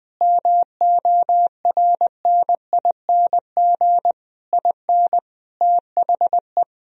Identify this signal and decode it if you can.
Morse code, heard as a keyed tone, about 20 wpm, 700 Hz. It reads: MORNING IN THE
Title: Sentences from Top 1000 Words - Review Part 3 - 20wpm